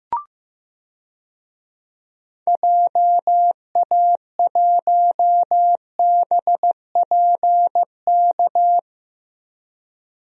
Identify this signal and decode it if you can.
Morse code, heard as a keyed tone, about 15 wpm, 700 Hz. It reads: JA1BPK
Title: DXCC-100 Call Signs - 6 Characters - Repeated 3x - 15wpm